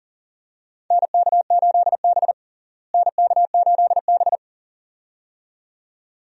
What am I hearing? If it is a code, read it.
Morse code, heard as a keyed tone, about 40 wpm, 700 Hz. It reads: NK8B NK8B
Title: USA Call Signs - 4 Characters - Repeated 2x - 40wpm